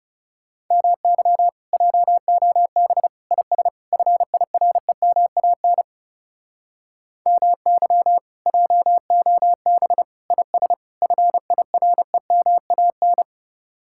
Morse code, heard as a keyed tone, about 40 wpm, 700 Hz. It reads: MY JOB IS FIREMAN MY JOB IS FIREMAN